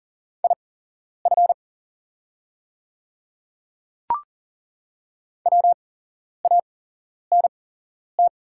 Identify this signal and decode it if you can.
Morse code, heard as a keyed tone, about 39 wpm, spaced as for 11 wpm, 690 Hz. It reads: IF WANT
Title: 8x Character Spacing - Top 300 Words - 40wpm